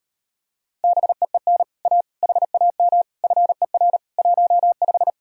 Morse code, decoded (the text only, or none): BEEN A HAM FER 15